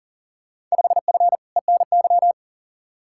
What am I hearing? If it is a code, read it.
Morse code, heard as a keyed tone, about 40 wpm, 700 Hz. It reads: 5F ENY
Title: Field Day Exchanges Encore - 40wpm